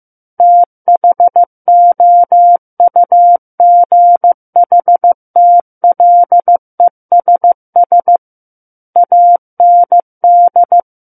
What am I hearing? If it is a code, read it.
Morse code, heard as a keyed tone, about 15 wpm, 700 Hz. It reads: THOUGHTLESS AND
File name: Chapter 6 - 15 WPM